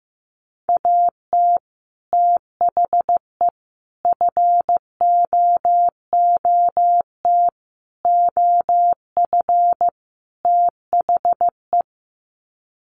Morse code, heard as a keyed tone, about 15 wpm, 700 Hz. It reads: AT THE FOOT OF THE
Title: Sets of 5 Words 15wpm